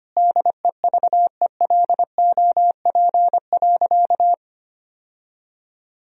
Morse code, heard as a keyed tone, about 25 wpm, 700 Hz. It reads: DEVELOP.
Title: Sentences from Top 1000 Words - Review Part 2 - 25wpm